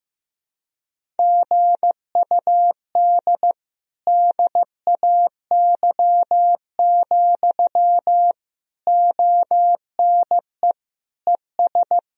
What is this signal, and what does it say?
Morse code, heard as a keyed tone, about 15 wpm, 700 Hz. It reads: GUD DAY, ONE ES
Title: Sentences from Top 200 Words and Abbreviations 15wpm